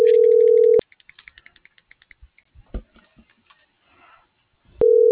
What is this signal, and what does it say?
Ringback tone (North America), 440 + 480 Hz.